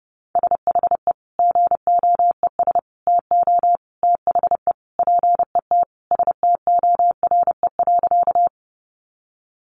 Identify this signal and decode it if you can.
Morse code, heard as a keyed tone, about 30 wpm, 700 Hz. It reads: SHE GOES TO THE PET STORE.
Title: Easy Phrases 30wpm